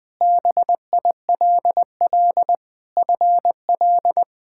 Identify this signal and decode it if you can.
Morse code, heard as a keyed tone, about 20 wpm, 700 Hz. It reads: BILL FL
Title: K1USN SST QSO Practice - 20wpm